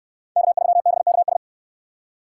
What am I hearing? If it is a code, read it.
Morse code, heard as a keyed tone, about 68 wpm, 700 Hz. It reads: K3BKS